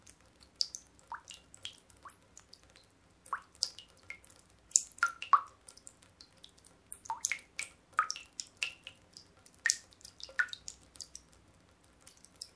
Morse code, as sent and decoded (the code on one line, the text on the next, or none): none